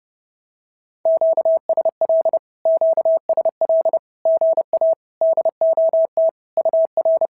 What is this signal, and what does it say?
Morse code, heard as a keyed tone, about 30 wpm, 650 Hz. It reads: QSL QSL GA DOT UR
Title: SOTA QSO Practice 30wpm